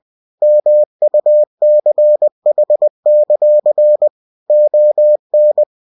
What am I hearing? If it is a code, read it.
Morse code, heard as a keyed tone, about 20 wpm, 600 Hz. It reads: MUCH; ON
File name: cw-books / winnie-the-pooh / winnie-20wpm-page-0102